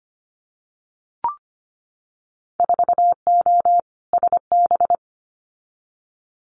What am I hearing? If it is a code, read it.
Morse code, heard as a keyed tone, about 25 wpm, 700 Hz. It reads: 4O SB
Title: Winter Field Day Exchanges 25wpm